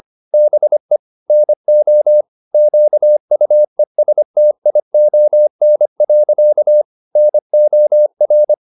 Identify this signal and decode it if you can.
Morse code, heard as a keyed tone, about 25 wpm, 600 Hz. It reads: BE NO QUESTION. NOR